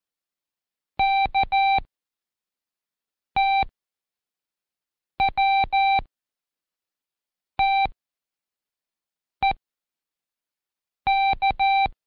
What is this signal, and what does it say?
Morse code, heard as a keyed tone, about 14 wpm, 780 Hz. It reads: K T W T E K
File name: Code Quick / AudioDownld1 / Seg11-Day2-ChTest2